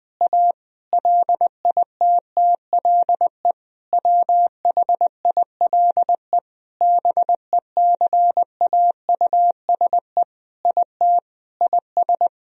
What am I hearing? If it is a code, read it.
Morse code, heard as a keyed tone, about 20 wpm, 700 Hz. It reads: A LITTLE WHILE BECAUSE IT IS
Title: Sentences from Top 1000 Words - Review Part 3 - 20wpm